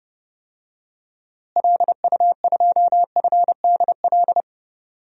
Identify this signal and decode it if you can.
Morse code, heard as a keyed tone, about 30 wpm, 700 Hz. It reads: LU2FDL
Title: DXCC-100 Call Signs - 6 Characters - Repeated 3x - 30wpm